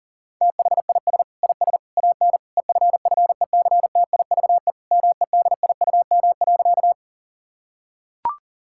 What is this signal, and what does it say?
Morse code, heard as a keyed tone, about 40 wpm, 700 Hz. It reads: THIS IS AN EFFECTIVE MEDIUM.
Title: Sentences from Top 1000 Words - Review Part 6 - 40wpm